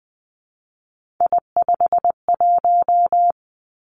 Morse code, heard as a keyed tone, about 20 wpm, 700 Hz. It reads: I51
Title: Three Challenging Letter-Number Combinations 20wpm